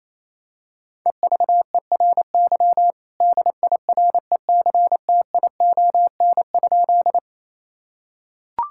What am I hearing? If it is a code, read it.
Morse code, heard as a keyed tone, about 28 wpm, 700 Hz. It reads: EVERY DIRECTION?
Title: Sentences from Top 1000 Words - Review Part 5 - 28wpm